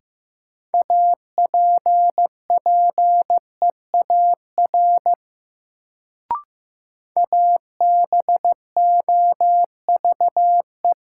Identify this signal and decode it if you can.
Morse code, heard as a keyed tone, about 15 wpm, 700 Hz. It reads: APPEAR ABOVE
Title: Sentences from Top 1200 Words - New Word Drill Alt - 15wpm